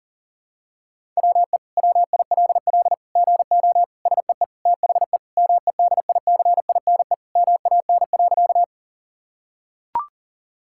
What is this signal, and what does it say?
Morse code, heard as a keyed tone, about 40 wpm, 700 Hz. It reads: WE WILL GO SEE THE MEDICINE MAN.
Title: Sentences from Top 1000 Words - Review Part 6 - 40wpm